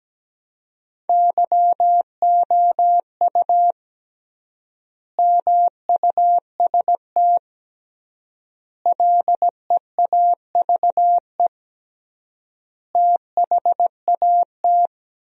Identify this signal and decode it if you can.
Morse code, heard as a keyed tone, about 17 wpm, 700 Hz. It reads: YOU MUST LEAVE THAT
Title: 3x Word-Spacing - Sentences from Top 400 Words - 17wpm